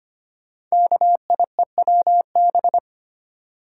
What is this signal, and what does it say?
Morse code, heard as a keyed tone, about 25 wpm, 700 Hz. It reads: KIEWB
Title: Code Groups - 5 Letters - 25wpm